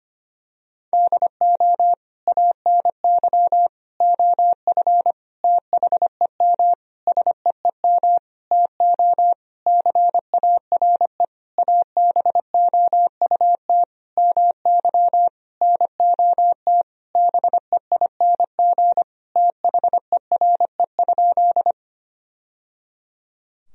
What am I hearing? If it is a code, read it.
Morse code, heard as a keyed tone, about 25 wpm, 700 Hz. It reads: DO ANY OF THEM SEEM TO CARE ABOUT MY NOT BEING THERE?
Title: Sentences from Top 1000 Words - Review Part 3 - 25wpm